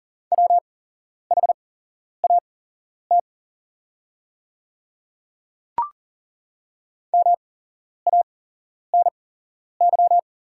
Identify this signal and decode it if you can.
Morse code, heard as a keyed tone, about 40 wpm, spaced as for 11 wpm, 700 Hz. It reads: WHAT MANY
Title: 8x Character Spacing - Top 100 Words - 40wpm